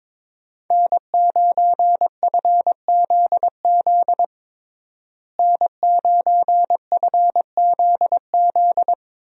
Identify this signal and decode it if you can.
Morse code, heard as a keyed tone, about 22 wpm, 700 Hz. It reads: N9FZZ N9FZZ
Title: USA Call Signs - 5 Characters - Repeated 3x - 22wpm